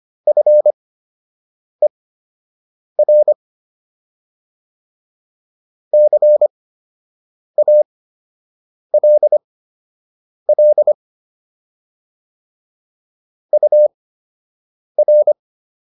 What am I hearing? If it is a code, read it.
Morse code, heard as a keyed tone, about 25 wpm, spaced as for 7 wpm, 600 Hz. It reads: FER CALL UR